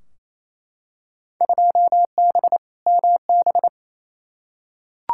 Morse code, heard as a keyed tone, about 28 wpm, 700 Hz. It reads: 2B MB
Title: Field Day Exchanges Encore - 28wpm